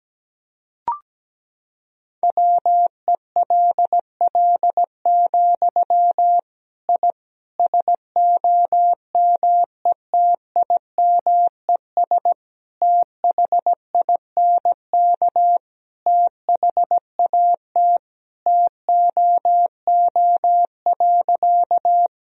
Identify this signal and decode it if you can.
Morse code, heard as a keyed tone, about 17 wpm, 700 Hz. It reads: WELL, I SOMETIMES THINK THAT TOO.
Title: Sentences from Top 1000 Words - Review Part 3 - 17wpm